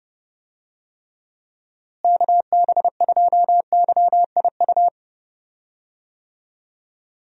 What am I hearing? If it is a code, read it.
Morse code, heard as a keyed tone, about 30 wpm, 700 Hz. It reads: KB2YIU
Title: USA Call Signs - 6 Characters - Repeated 3x - 30wpm